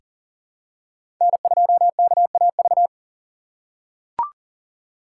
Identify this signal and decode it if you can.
Morse code, heard as a keyed tone, about 40 wpm, 700 Hz. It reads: N2KAV
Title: USA Call Signs - 5 Characters - 2x Character Spacing - 40wpm